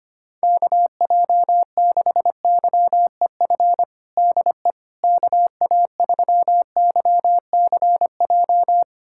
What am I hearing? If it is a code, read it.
Morse code, heard as a keyed tone, about 25 wpm, 700 Hz. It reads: KJ6YEF DE KA3YCJ
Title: USA Callsign DE Callsign 25wpm